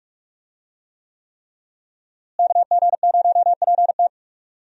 Morse code, heard as a keyed tone, about 46 wpm, 700 Hz. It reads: KG0PT
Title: USA Call Signs - 5 Characters - Repeated 3x - 45wpm